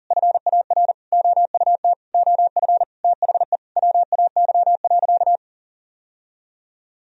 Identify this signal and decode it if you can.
Morse code, heard as a keyed tone, about 40 wpm, 700 Hz. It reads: FAR OUT OF THE WAY.
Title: Sentences from Top 1000 Words - Review Part 1 - 40wpm